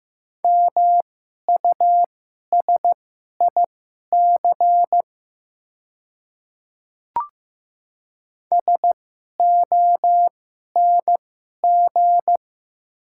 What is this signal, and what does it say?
Morse code, heard as a keyed tone, about 15 wpm, spaced as for 11 wpm, 700 Hz. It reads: MUSIC SONG
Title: 2x Character Spacing - Top 500 Words - 15wpm